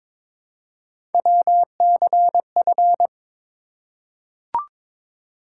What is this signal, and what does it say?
Morse code, heard as a keyed tone, about 22 wpm, 700 Hz. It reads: WCF
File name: ARRL Sections v2 - 22wpm